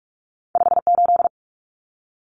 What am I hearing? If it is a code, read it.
Morse code, heard as a keyed tone, about 44 wpm, 700 Hz. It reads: 58
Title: Numbers - 2-Digit 45wpm